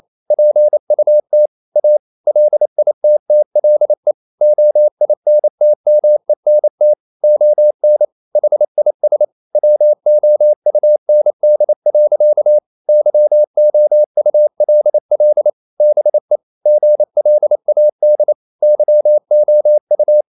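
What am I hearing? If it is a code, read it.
Morse code, heard as a keyed tone, about 28 wpm, 600 Hz. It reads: PUT A LITTLE OINTMENT ON HIS WOUND. YOULL BE GLAD YOU
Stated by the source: Morse code practice at 28 wpm, text of Ken Follett's novel Never in English, from CW med Gnister.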